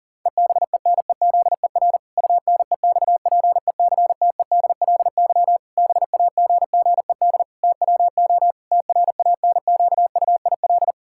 Morse code, heard as a keyed tone, about 40 wpm, 700 Hz. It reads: EBENEZER UNEXPECTEDLY BAGGED TWO TRANQUIL
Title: Pangram Challenge v2 - 40wpm